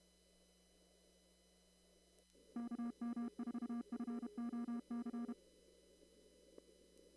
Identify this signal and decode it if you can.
Morse code, heard as a keyed tone, about 32 wpm, 240 Hz. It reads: KM4FOC
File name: CW / kt5j wpx cw-op-k5pi